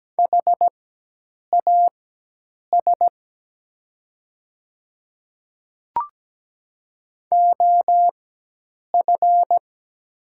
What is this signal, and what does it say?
Morse code, heard as a keyed tone, about 17 wpm, spaced as for 8 wpm, 700 Hz. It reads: HAS OF
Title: 4x Character Spacing - Top 300 Words - 17wpm